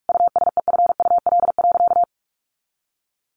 Morse code, heard as a keyed tone, about 44 wpm, 700 Hz. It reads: USEFUL.
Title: Sentences from Top 700 Words 45wpm